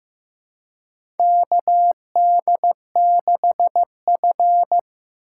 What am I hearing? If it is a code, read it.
Morse code, heard as a keyed tone, about 15 wpm, 700 Hz. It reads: KD6F
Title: USA Call Signs - 4 Characters - 4x Character Spacing - 15wpm